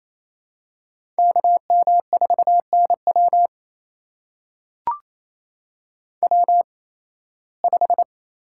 Morse code, heard as a keyed tone, about 28 wpm, spaced as for 18 wpm, 700 Hz. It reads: KM4NW W5
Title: USA Call Signs - 5 Characters - 8x Character Spacing - 28wpm